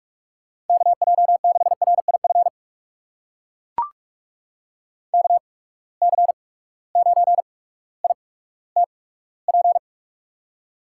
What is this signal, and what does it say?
Morse code, heard as a keyed tone, about 44 wpm, spaced as for 19 wpm, 700 Hz. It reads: KJ6RIF KC9ITP
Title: USA Call Signs - 6 Characters - 8x Character Spacing - 45wpm